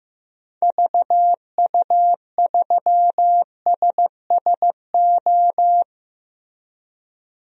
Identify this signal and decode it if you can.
Morse code, heard as a keyed tone, about 15 wpm, 700 Hz. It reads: VU3SSO
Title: DXCC-100 Call Signs - 6 Characters - Repeated 2x - 15wpm